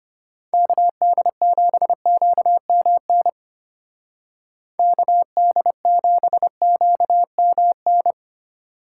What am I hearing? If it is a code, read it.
Morse code, heard as a keyed tone, about 31 wpm, 700 Hz. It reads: KD7QMN KD7QMN
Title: Speed Racing - USA 2x3 Call Signs 25wpm